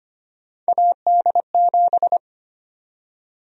Morse code, heard as a keyed tone, about 25 wpm, 700 Hz. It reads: AD7
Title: USA Call Sign Prefixes - 25wpm